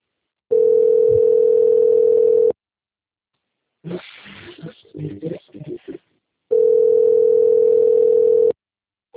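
Ringback tone (North America), 440 + 480 Hz.
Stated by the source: Outgoing telephone call recorded on the Limeira V system.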